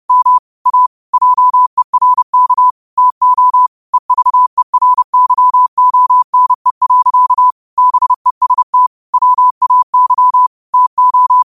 Morse code, heard as a keyed tone, about 30 wpm, 1000 Hz. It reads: M A JERK TO EVERYONE. BEST WAY TO
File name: qotdfiles / 30wpm